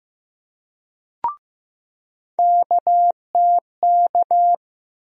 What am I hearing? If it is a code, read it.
Morse code, heard as a keyed tone, about 15 wpm, 700 Hz. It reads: KTK
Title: USA Call Sign Postfixes - 3 Letters 15wpm